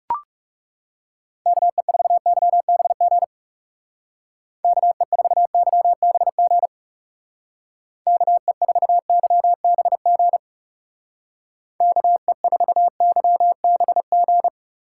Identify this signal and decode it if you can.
Morse code, heard as a keyed tone, about 36 wpm, 700 Hz. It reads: KE4YBG KE4YBG KE4YBG KE4YBG
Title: Speed Racing - USA 2x3 Call Signs 30wpm